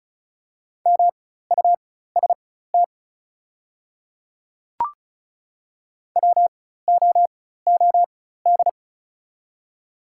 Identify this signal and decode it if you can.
Morse code, heard as a keyed tone, about 36 wpm, spaced as for 16 wpm, 690 Hz. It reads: MUST WOOD
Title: 4x Character Spacing - Top 300 Words - 35wpm